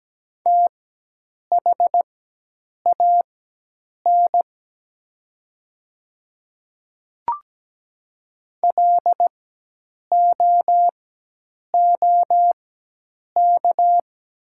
Morse code, heard as a keyed tone, about 17 wpm, spaced as for 8 wpm, 700 Hz. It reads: THAN LOOK
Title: 4x Character Spacing - Top 100 Words - 17wpm